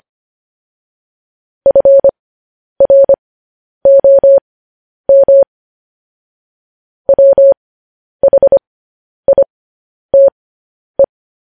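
Morse code, heard as a keyed tone, about 25 wpm, spaced as for 10 wpm, 550 Hz. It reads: FROM WHITE
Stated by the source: Morse code news headlines.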